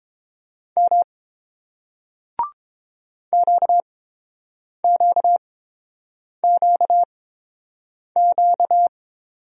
Morse code, heard as a keyed tone, about 30 wpm, spaced as for 7 wpm, 700 Hz. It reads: M QQQQ